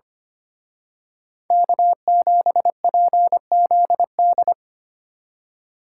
Morse code, heard as a keyed tone, about 29 wpm, 700 Hz. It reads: K7PZD
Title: Speed Racing - USA 1x3 Call Signs 25wpm